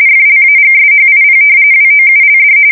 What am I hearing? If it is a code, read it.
RTTY, about 45.45 baud. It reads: FLAGSHIP WOULD
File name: fsk170 / battleshipnj